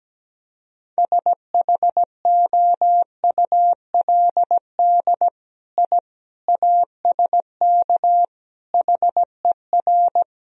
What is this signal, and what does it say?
Morse code, heard as a keyed tone, about 17 wpm, 700 Hz. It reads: SHOULD I ASK HER
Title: Sentences from Top 1000 Words - Review Part 4 - 17wpm